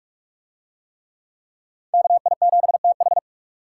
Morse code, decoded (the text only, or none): KI7TH